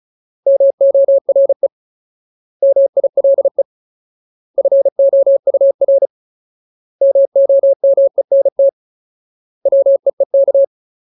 Morse code, heard as a keyed tone, about 34 wpm, 550 Hz. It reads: MORE MILE FOUR MOMENT WEEK
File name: Ultimate Word List / 35WPM Ultimate